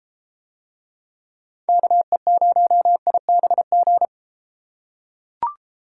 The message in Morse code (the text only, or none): KE0IBG